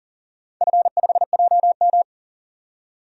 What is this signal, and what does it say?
Morse code, heard as a keyed tone, about 40 wpm, 700 Hz. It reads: F5JM
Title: DXCC-100 Call Signs - 4 Characters - 40wpm